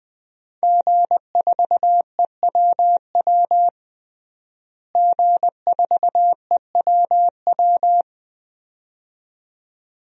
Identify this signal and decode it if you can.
Morse code, heard as a keyed tone, about 20 wpm, 700 Hz. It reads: G4EWW G4EWW
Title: DXCC-100 Call Signs - 5 Characters - Repeated 2x - 20wpm